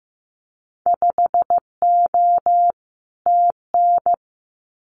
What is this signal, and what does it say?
Morse code, heard as a keyed tone, about 15 wpm, 700 Hz. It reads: 5O TN
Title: Winter Field Day Exchanges Encore 15wpm